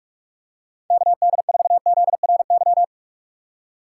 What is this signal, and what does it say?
Morse code, heard as a keyed tone, about 46 wpm, 700 Hz. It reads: KD4ZRY
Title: USA Call Signs - 2x Character Spacing - 45wpm